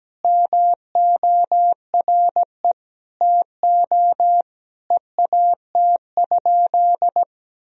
Morse code, heard as a keyed tone, about 17 wpm, 700 Hz. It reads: MORE TO EAT?
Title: Sentences from Top 300 Words and Abbreviations 17wpm